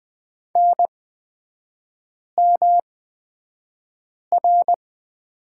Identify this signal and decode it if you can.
Morse code, heard as a keyed tone, about 20 wpm, spaced as for 5 wpm, 700 Hz. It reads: NMR
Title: 229 Lesson 37e - Letter x - Three Characters 20wpm